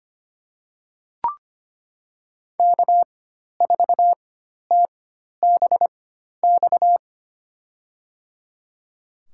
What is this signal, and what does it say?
Morse code, heard as a keyed tone, about 25 wpm, spaced as for 12 wpm, 700 Hz. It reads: K4TBX
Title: USA Call Signs - 5 Characters - 4x Character Spacing - 25wpm